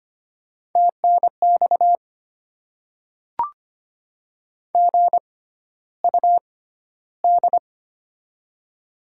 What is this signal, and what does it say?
Morse code, heard as a keyed tone, about 25 wpm, spaced as for 13 wpm, 700 Hz. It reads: TNX GUD
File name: Top 20 QSO Elements - 6x Character Spacing - 25wpm